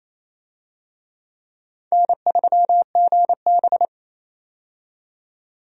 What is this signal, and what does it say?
Morse code, heard as a keyed tone, about 29 wpm, 700 Hz. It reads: N3GB
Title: USA Call Signs - 4 Characters - 28wpm